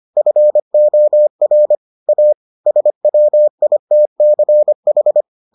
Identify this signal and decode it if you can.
Morse code, heard as a keyed tone, about 25 wpm, 600 Hz. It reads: FOR A SWITCH